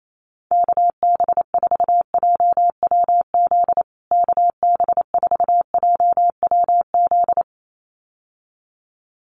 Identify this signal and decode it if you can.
Morse code, heard as a keyed tone, about 28 wpm, 700 Hz. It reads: KB4JWZ KB4JWZ
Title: USA Call Signs Repeated 2x 28wpm